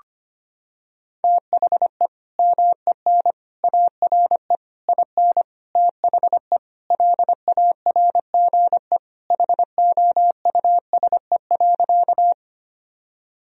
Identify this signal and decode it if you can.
Morse code, heard as a keyed tone, about 25 wpm, 700 Hz. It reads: THE MEN ARE IN THE LARGE HOUSE.
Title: Sentences from Top 1000 Words - Review Part 1 - 25wpm